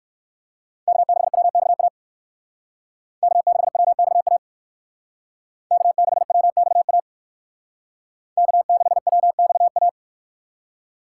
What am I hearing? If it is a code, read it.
Morse code, heard as a keyed tone, about 55 wpm, 700 Hz. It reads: K6WXA K6WXA K6WXA K6WXA
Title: Speed Racing - USA 1x3 Call Signs 45wpm